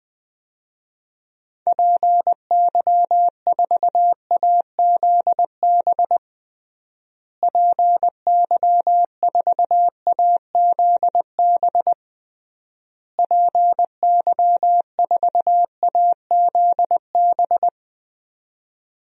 Morse code, heard as a keyed tone, about 20 wpm, 700 Hz. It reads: PY4AZB PY4AZB PY4AZB
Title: DXCC-100 Call Signs - 6 Characters - Repeated 3x - 20wpm